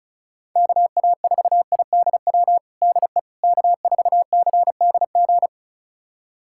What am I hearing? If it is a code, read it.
Morse code, heard as a keyed tone, about 34 wpm, 700 Hz. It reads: KA4IDW DE K4CDG
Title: USA Callsign DE Callsign 35wpm